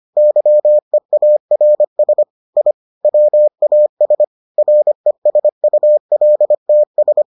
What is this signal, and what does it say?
Morse code, heard as a keyed tone, about 25 wpm, 600 Hz. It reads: YEARS I WAS RESULTS